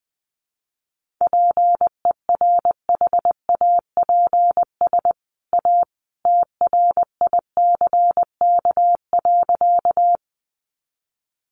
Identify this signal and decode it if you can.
Morse code, heard as a keyed tone, about 20 wpm, 700 Hz. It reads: PERHAPS A TRICK.
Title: Sentences from Top 1500 Words 20wpm